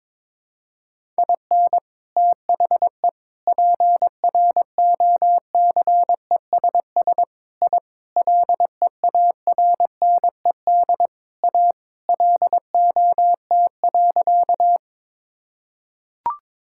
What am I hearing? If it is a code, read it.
Morse code, heard as a keyed tone, about 22 wpm, 700 Hz. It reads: IN THE PROCESS I LEARNED A LOT.
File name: practice / 22wpm / Sentences with 3-Word Phrases - 22wpm